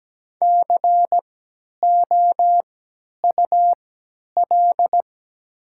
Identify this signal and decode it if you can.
Morse code, heard as a keyed tone, about 17 wpm, spaced as for 10 wpm, 690 Hz. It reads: COUL
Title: 3x Character Spacing - Top 200 Words - 17wpm